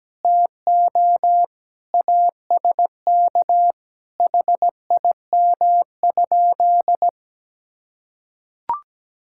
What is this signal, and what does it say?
Morse code, heard as a keyed tone, about 17 wpm, 700 Hz. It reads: TO ASK HIM?
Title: Sentences from Top 1600 Words 17wpm